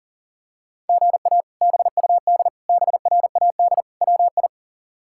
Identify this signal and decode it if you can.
Morse code, heard as a keyed tone, about 40 wpm, 700 Hz. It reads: GA BUD BRAD WI
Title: K1USN SST QSO Practice - 40wpm